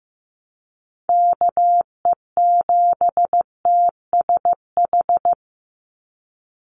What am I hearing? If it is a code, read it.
Morse code, heard as a keyed tone, about 15 wpm, 700 Hz. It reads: KE7TSH
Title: USA Call Signs 15wpm